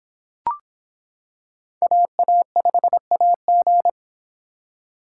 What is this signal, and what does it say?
Morse code, heard as a keyed tone, about 24 wpm, 700 Hz. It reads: AA5AG